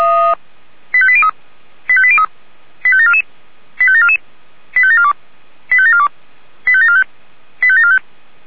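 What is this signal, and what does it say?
Five-tone selective call (ZVEI), heard as quick runs of five tones, tones of about 70 ms. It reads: 85803 85803 85744 85744 85742 85742 85745 85745
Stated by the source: ZVEI five-tone call 85730 received via SDR.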